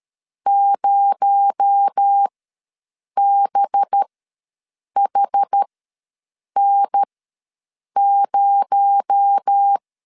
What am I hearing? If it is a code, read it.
Morse code, heard as a keyed tone, about 13 wpm, spaced as for 7 wpm, 800 Hz. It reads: 0BHN0
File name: LEZIONI / 6 PKX0 / 6 PKX0 ws 12wpm 3448